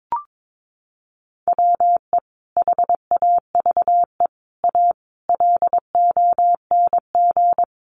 Morse code, heard as a keyed tone, about 22 wpm, 700 Hz. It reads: WE HAVE A LONG
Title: Sentences from Top 1200 Words 22wpm